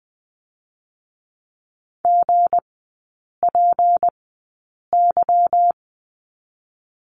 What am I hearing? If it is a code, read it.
Morse code, heard as a keyed tone, about 20 wpm, spaced as for 8 wpm, 690 Hz. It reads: GPY
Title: 159 Lesson 26e - Letter g - Three Characters 20wpm